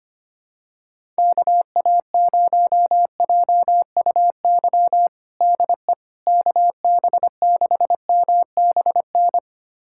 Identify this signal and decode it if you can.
Morse code, heard as a keyed tone, about 25 wpm, 700 Hz. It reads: KA0JUY DE KB6MBN
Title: USA Callsign DE Callsign 25wpm